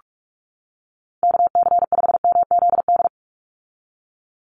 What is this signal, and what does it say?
Morse code, heard as a keyed tone, about 44 wpm, 700 Hz. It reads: KC5MZD
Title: USA Call Signs 45wpm